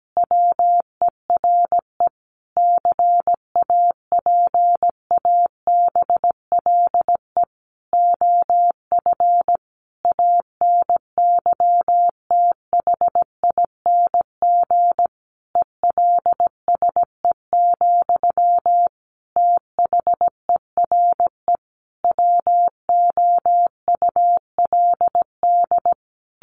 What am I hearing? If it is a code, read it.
Morse code, heard as a keyed tone, about 17 wpm, 700 Hz. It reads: WERE CAPABLE OF ANYTHING ELSE, THERE WOULD
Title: Sentences from Top 1200 Words 17wpm